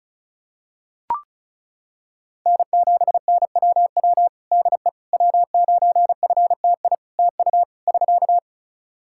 Morse code, heard as a keyed tone, about 36 wpm, 700 Hz. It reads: N7NWW DE W9FTI TU <SK>
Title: Practice QSO - Exchange 4 - Ending - 35wpm